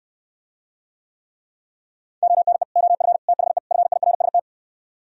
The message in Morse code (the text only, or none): ONE OF THE FIRST